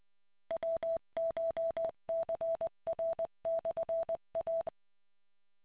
Morse code, heard as a keyed tone, about 20 wpm, 660 Hz. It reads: WCHCR/A